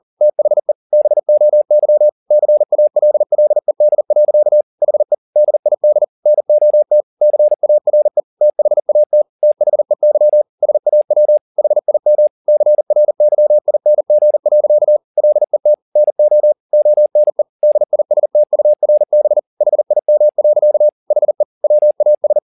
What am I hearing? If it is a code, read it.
Morse code, heard as a keyed tone, about 40 wpm, 600 Hz. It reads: THE BOY CALLED. HE DID NOT CARE THAT THEY SAW HIM CRYING. LET NO ONE DISTURB HIM. HE WAS
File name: the old man and the sea 2 40wpm